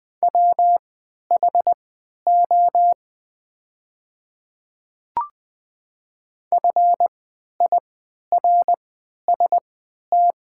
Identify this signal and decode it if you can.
Morse code, heard as a keyed tone, about 21 wpm, spaced as for 11 wpm, 700 Hz. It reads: WHO FIRST
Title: 3x Character Spacing - Top 100 Words - 20wpm